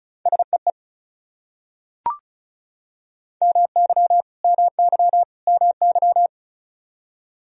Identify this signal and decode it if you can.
Morse code, heard as a keyed tone, about 36 wpm, 700 Hz. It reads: SEE MY MY MY
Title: Top 100 Words Repeated 35wpm